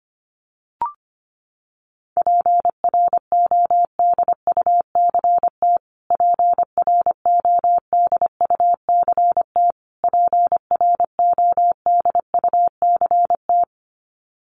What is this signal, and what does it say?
Morse code, heard as a keyed tone, about 25 wpm, 700 Hz. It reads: PRODUCT PRODUCT PRODUCT
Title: Top 500 Words Repeated 25wpm